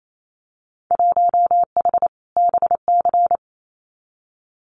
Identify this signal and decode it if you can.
Morse code, heard as a keyed tone, about 28 wpm, 700 Hz. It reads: 1H BC
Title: Winter Field Day Exchanges 28wpm